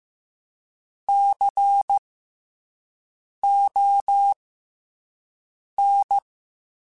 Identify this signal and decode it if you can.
Morse code, heard as a keyed tone, about 15 wpm, spaced as for 5 wpm, 800 Hz. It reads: CON
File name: Special INGs-mp3s / Special INGs-mp3s / SlowToFaster ing @10